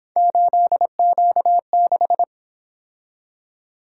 Morse code, heard as a keyed tone, about 28 wpm, 700 Hz. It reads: 8Q6